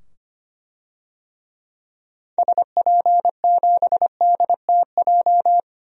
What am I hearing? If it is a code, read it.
Morse code, heard as a keyed tone, about 25 wpm, 700 Hz. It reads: SP7DTJ
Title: DXCC-100 Call Signs - 6 Characters - 25wpm